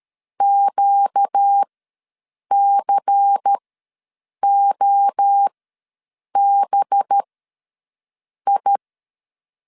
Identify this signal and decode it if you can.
Morse code, heard as a keyed tone, about 13 wpm, spaced as for 7 wpm, 800 Hz. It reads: QCOBI
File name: LEZIONI / 5 LBVC / 5 LBVC ws 12wpm 3159